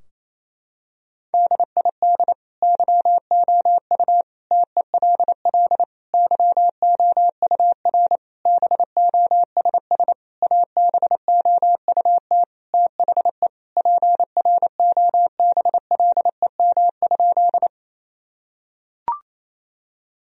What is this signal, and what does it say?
Morse code, heard as a keyed tone, about 27 wpm, 700 Hz. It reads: DID YOU TELL YOUR BOSS ABOUT THE PROBLEM?
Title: Sentences from Top 1000 Words - Review Part 4 - 28wpm